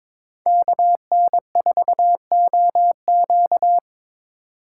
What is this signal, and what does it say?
Morse code, heard as a keyed tone, about 22 wpm, 700 Hz. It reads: KN4OQ
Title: USA Call Signs - 5 Characters - 6x Character Spacing - 22wpm